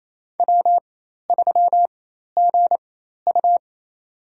Morse code, heard as a keyed tone, about 28 wpm, spaced as for 13 wpm, 700 Hz. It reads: W3GU
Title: USA Call Signs - 4x Character Spacing - 28wpm